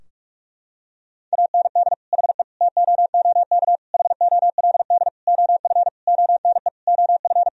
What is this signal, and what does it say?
Morse code, heard as a keyed tone, about 45 wpm, 700 Hz. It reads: AND HE TOOK HOLD OF ONE OF
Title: Sentences from Top 1000 Words - Review Part 4 - 45wpm